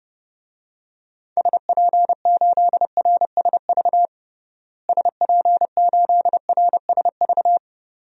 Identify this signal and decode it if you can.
Morse code, heard as a keyed tone, about 32 wpm, 700 Hz. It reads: SP8RSV SP8RSV
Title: DXCC-100 Call Signs - 6 Characters - Repeated 2x - 30wpm